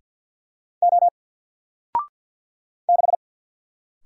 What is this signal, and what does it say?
Morse code, heard as a keyed tone, about 50 wpm, 700 Hz. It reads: O 6
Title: Single Letter-Number Alt 50wpm